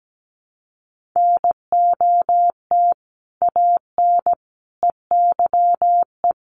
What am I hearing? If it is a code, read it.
Morse code, heard as a keyed tone, about 17 wpm, 700 Hz. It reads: NOT AN EYE
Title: Sentences from Top 1400 Words 17wpm